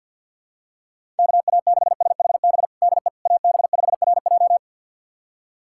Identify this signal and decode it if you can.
Morse code, heard as a keyed tone, about 51 wpm, 700 Hz. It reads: KA6SHB DE AB5RJ
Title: USA Callsign DE Callsign 50wpm